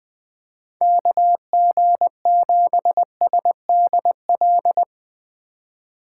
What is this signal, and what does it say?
Morse code, heard as a keyed tone, about 21 wpm, 700 Hz. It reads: KG7SDL
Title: Speed Racing - USA Call Signs 17wpm